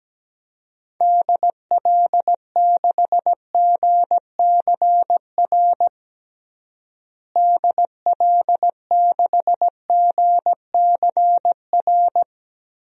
Morse code, heard as a keyed tone, about 17 wpm, 700 Hz. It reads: DL6GCR DL6GCR
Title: DXCC-100 Call Signs - 6 Characters - Repeated 3x - 17wpm